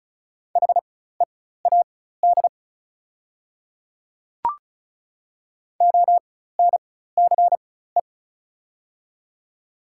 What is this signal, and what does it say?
Morse code, heard as a keyed tone, about 36 wpm, spaced as for 16 wpm, 700 Hz. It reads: HEAD ONCE
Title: 4x Character Spacing - Top 300 Words - 35wpm